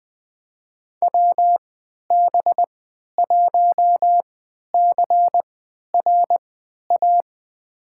Morse code, heard as a keyed tone, about 20 wpm, spaced as for 11 wpm, 700 Hz. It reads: WB1CRA
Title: USA Call Signs - 6 Characters - 3x Character Spacing - 20wpm